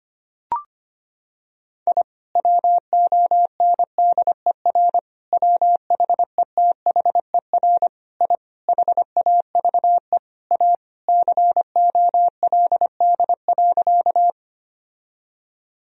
Morse code, heard as a keyed tone, about 25 wpm, 700 Hz. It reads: I WONDER WHETHER I HAVE A COLD.
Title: Sentences from Top 1000 Words - Review Part 3 - 25wpm